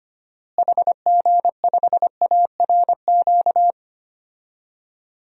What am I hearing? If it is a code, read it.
Morse code, heard as a keyed tone, about 25 wpm, 700 Hz. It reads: HG5ARQ
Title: DXCC-100 Call Signs - 6 Characters - Repeated 2x - 25wpm